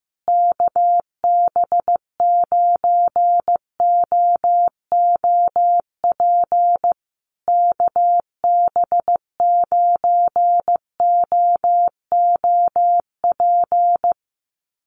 Morse code, heard as a keyed tone, about 15 wpm, 700 Hz. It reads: KB9OOP KB9OOP
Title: USA Call Signs Repeated 3x 15wpm